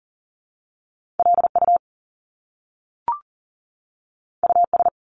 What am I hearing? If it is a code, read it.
Morse code, heard as a keyed tone, about 40 wpm, 700 Hz. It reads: LU US
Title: Top Two Letter Combinations in Words 40wpm